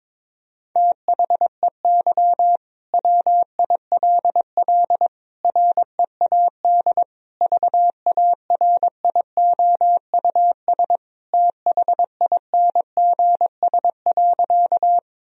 Morse code, heard as a keyed tone, about 22 wpm, 700 Hz. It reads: THEY WILL READ VARIOUS THINGS.
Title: Sentences from Top 1000 Words - Review Part 4 - 22wpm